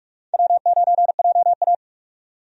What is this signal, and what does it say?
Morse code, heard as a keyed tone, about 42 wpm, 700 Hz. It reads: W9JA
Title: USA Call Signs - 4 Characters - 45wpm